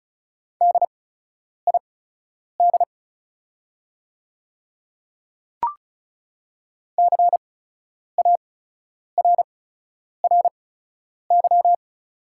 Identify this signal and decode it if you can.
Morse code, heard as a keyed tone, about 36 wpm, spaced as for 10 wpm, 690 Hz. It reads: DID CARRY
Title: 8x Character Spacing - Top 300 Words - 35wpm